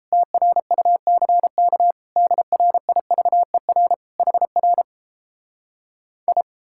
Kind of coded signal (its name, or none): Morse code